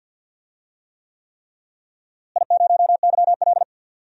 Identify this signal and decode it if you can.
Morse code, heard as a keyed tone, about 53 wpm, 700 Hz. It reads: I0YL
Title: DXCC-100 Call Signs - 4 Characters - Repeated 3x - 50wpm